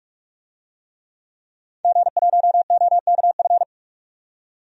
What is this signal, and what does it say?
Morse code, heard as a keyed tone, about 44 wpm, 700 Hz. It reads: G1OKF
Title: DXCC-100 Call Signs - 5 Characters - 45wpm